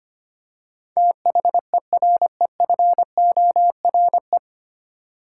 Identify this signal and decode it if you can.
Morse code, heard as a keyed tone, about 25 wpm, 700 Hz. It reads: THEREFORE
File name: practice / 25wpm / 9 Letter Words - First 3 Letters Given - 25wpm